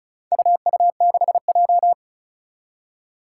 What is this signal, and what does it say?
Morse code, heard as a keyed tone, about 36 wpm, 700 Hz. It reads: UU6J
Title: Four Challenging Letter-Number Combinations 35wpm